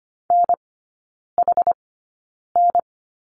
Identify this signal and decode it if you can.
Morse code, heard as a keyed tone, about 25 wpm, spaced as for 9 wpm, 700 Hz. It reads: NHN
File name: course / 25:10wpm / 093 Lesson 16e - Letter c - Three Characters 25wpm